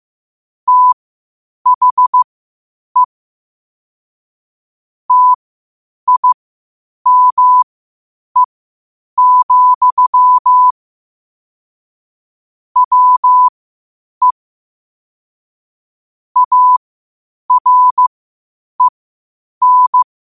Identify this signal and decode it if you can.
Morse code, heard as a keyed tone, about 15 wpm, spaced as for 8 wpm, 1000 Hz. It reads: THE TIME, WE AREN
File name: qotdfiles / 8wpm